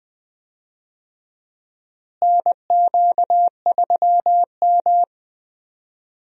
Morse code, heard as a keyed tone, about 20 wpm, 700 Hz. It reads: NQ3M
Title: USA Call Signs - 4 Characters - Repeated 2x - 20wpm